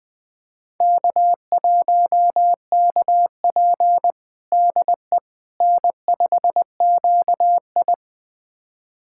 Morse code, heard as a keyed tone, about 20 wpm, 700 Hz. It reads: K1KP DE N5QI
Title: USA Callsign DE Callsign - 4 Characters - 20wpm